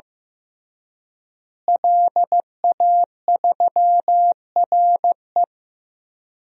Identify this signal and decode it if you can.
Morse code, heard as a keyed tone, about 15 wpm, 700 Hz. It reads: LA3RE
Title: DXCC-100 Call Signs - 5 Characters - Repeated 2x - 15wpm